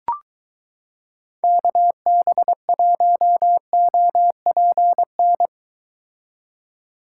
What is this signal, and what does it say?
Morse code, heard as a keyed tone, about 22 wpm, 700 Hz. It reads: KB1OPN